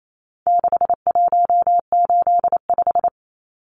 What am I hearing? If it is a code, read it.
Morse code, heard as a keyed tone, about 28 wpm, 700 Hz. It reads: 6185
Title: Numbers - 4-Digit 28wpm